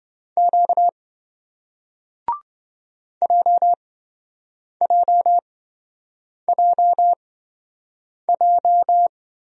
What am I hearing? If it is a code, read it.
Morse code, heard as a keyed tone, about 26 wpm, spaced as for 7 wpm, 700 Hz. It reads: Q JJJJ